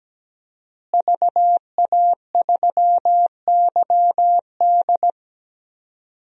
Morse code, heard as a keyed tone, about 17 wpm, 700 Hz. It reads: VA3YD
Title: DXCC-100 Call Signs - 5 Characters - Repeated 2x - 17wpm